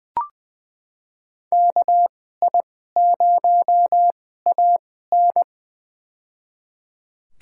Morse code, heard as a keyed tone, about 20 wpm, spaced as for 15 wpm, 700 Hz. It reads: KI0AN
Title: USA Call Signs - 2x Character Spacing - 20wpm